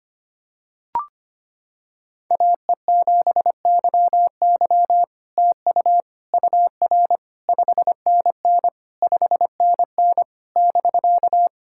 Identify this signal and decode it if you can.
Morse code, heard as a keyed tone, about 25 wpm, 700 Hz. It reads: AE7YY TU UR 5NN 5NN <BK>
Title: SOTA QSO Practice 25wpm